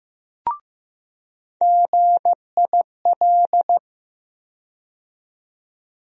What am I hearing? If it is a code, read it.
Morse code, heard as a keyed tone, about 15 wpm, 700 Hz. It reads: GIL
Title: USA Call Sign Postfixes - 3 Letters 15wpm